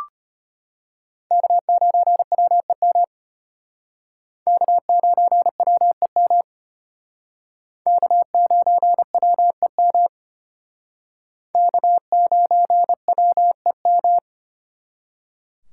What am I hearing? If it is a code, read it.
Morse code, heard as a keyed tone, about 34 wpm, 700 Hz. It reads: K9WEM K9WEM K9WEM K9WEM